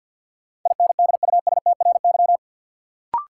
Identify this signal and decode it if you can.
Morse code, heard as a keyed tone, about 47 wpm, 700 Hz. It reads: INDUSTRY
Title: Sentences from Top 1300 Words - New Word Drill Alt - 50wpm